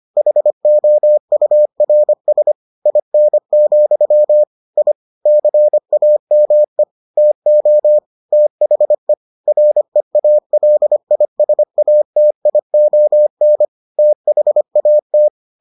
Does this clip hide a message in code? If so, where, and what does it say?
Yes: Morse code from 0.2 to 15.3 s, heard as a keyed tone, about 25 wpm, 600 Hz. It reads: HOURS IN, I CAME TO THE REALISATION THAT